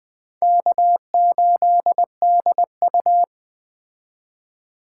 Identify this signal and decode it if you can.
Morse code, heard as a keyed tone, about 20 wpm, 700 Hz. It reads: K8DU